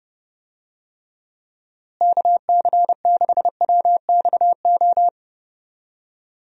Morse code, heard as a keyed tone, about 30 wpm, 700 Hz. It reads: KC6WXO
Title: USA Call Signs - 6 Characters - Repeated 2x - 30wpm